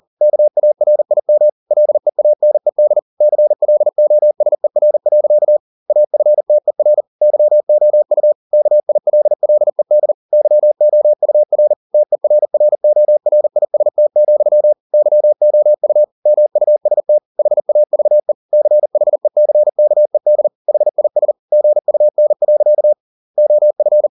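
Morse code, heard as a keyed tone, about 40 wpm, 600 Hz. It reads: KARIM LEANED CLOSER. AFTER YOU KILLED YOUR TERRORIST, YOU MUST HAVE CHECKED HIS GUN. OF
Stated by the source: Morse code at 40 wpm, practice text, CW med Gnister.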